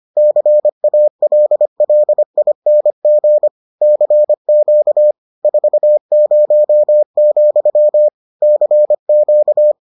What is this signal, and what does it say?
Morse code, heard as a keyed tone, about 25 wpm, 600 Hz. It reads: CALLING CQ 40, CQ